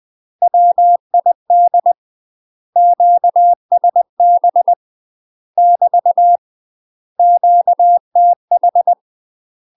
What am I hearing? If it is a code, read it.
Morse code, heard as a keyed tone, about 20 wpm, 700 Hz. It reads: WID QSB <BT> QTH
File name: UK MASTER QSOs TRAINING / 20WPM / UK MASTER QSOs 0049